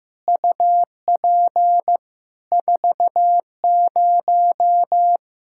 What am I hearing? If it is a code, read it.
Morse code, heard as a keyed tone, about 15 wpm, 700 Hz. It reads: UP 40
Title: K1USN SST QSO Practice 15wpm